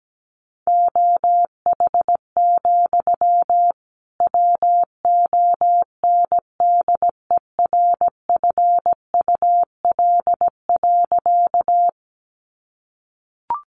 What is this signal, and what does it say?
Morse code, heard as a keyed tone, about 17 wpm, 700 Hz. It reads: OH, WONDERFUL.
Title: Sentences from Top 1100 Words 17wpm